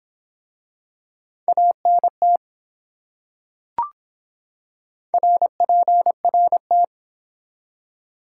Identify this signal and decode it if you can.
Morse code, heard as a keyed tone, about 24 wpm, 700 Hz. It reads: ANT RPRT